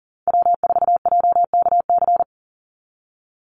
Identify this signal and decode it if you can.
Morse code, heard as a keyed tone, about 38 wpm, 700 Hz. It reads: W4JKC
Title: NA SOTA Chaser Callsigns 40wpm